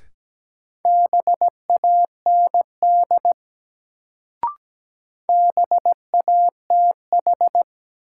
Morse code, heard as a keyed tone, about 17 wpm, 700 Hz. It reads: BAND BATH
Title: Sentences from Top 1600 Words - New Word Drill - 17wpm